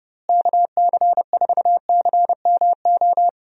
Morse code, heard as a keyed tone, about 30 wpm, 700 Hz. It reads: KC4CMO
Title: USA Call Signs - 6 Characters - Repeated 3x - 30wpm